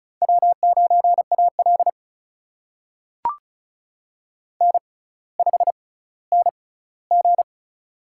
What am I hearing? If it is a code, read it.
Morse code, heard as a keyed tone, about 36 wpm, spaced as for 18 wpm, 700 Hz. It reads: W9AL N5NG
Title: USA Call Signs - 4 Characters - 6x Character Spacing - 35wpm